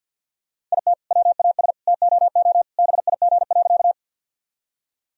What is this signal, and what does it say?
Morse code, heard as a keyed tone, about 50 wpm, 700 Hz. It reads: IT WAS TOO BIG.
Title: Sentences from Top 1000 Words - Review Part 1 - 50wpm